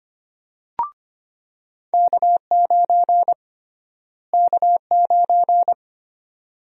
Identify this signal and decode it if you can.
Morse code, heard as a keyed tone, about 25 wpm, 700 Hz. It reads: K9 K9
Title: USA Call Sign Prefixes - Repeated 2x - 25wpm